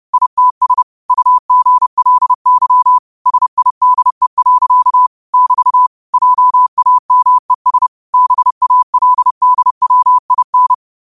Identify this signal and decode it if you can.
Morse code, heard as a keyed tone, about 30 wpm, 1000 Hz. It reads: ITS UGLY SIDE. <BT> JAMES BALDWIN
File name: qotdfiles / 30wpm